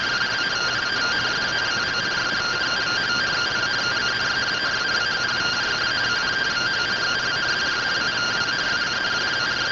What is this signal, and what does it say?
Two-tone teleprinter signal, about 170 Hz shift.